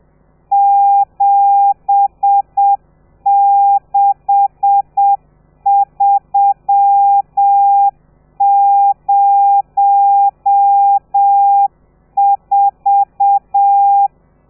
Morse code, heard as a keyed tone, about 7 wpm, 800 Hz. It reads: 76304
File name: telegrafia / i2tfj / Lez13 30